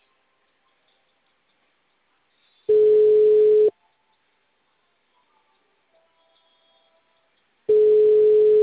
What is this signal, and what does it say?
Ringback tone (Europe, 425 Hz).